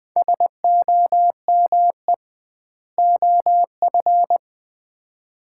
Morse code, heard as a keyed tone, about 20 wpm, 700 Hz. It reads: SOME OF
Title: 2x Word-Spacing - Sets of 2 Words - 20wpm